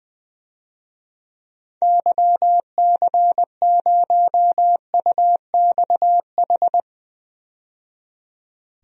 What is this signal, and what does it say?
Morse code, heard as a keyed tone, about 20 wpm, 700 Hz. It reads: YC0UXH
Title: DXCC-100 Call Signs - 6 Characters - 20wpm